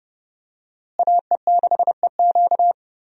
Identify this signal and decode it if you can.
Morse code, heard as a keyed tone, about 30 wpm, 700 Hz. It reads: AE6EQ
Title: USA Call Signs - 5 Characters - Repeated 3x - 30wpm